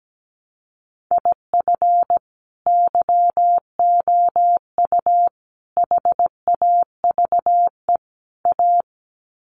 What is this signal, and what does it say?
Morse code, heard as a keyed tone, about 17 wpm, 700 Hz. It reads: IF YOU HAVE A
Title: Sets of 4 Words 17wpm